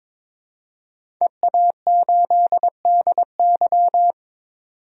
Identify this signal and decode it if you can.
Morse code, heard as a keyed tone, about 22 wpm, 700 Hz. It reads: EA8DY
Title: DXCC-100 Call Signs - 5 Characters - 22wpm